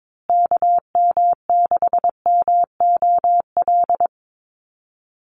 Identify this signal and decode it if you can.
Morse code, heard as a keyed tone, about 22 wpm, 700 Hz. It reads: KM6MOL
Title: USA Call Signs Repeated 2x 22wpm